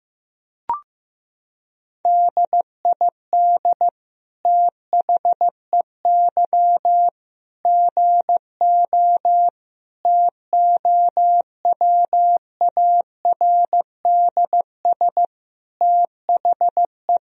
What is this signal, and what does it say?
Morse code, heard as a keyed tone, about 15 wpm, 700 Hz. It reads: DID THEY GO TOWARDS THE
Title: Sentences from Top 1000 Words - Review Part 2 - 15wpm